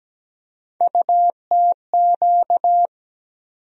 Morse code, heard as a keyed tone, about 17 wpm, 700 Hz. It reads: UTQ
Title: USA Call Sign Postfixes - 3 Letters 17wpm